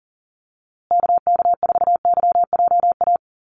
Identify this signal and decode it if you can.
Morse code, heard as a keyed tone, about 40 wpm, 700 Hz. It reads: KK4YJA
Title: USA Call Signs Repeated 2x 40wpm